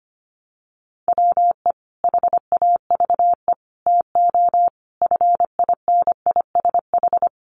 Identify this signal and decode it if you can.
Morse code, heard as a keyed tone, about 25 wpm, 700 Hz. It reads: WE HAVE TO FINISH
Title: Sentences from Top 1200 Words 25wpm